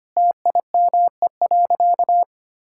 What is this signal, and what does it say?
Morse code, heard as a keyed tone, about 25 wpm, 700 Hz. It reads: TIME.
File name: practice / 25wpm / Sentences with 4-Word Phrases - 25wpm